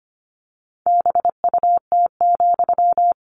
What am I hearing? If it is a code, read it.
Morse code, heard as a keyed tone, about 25 wpm, 700 Hz. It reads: BUT,
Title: Sentences from Top 600 Words 25wpm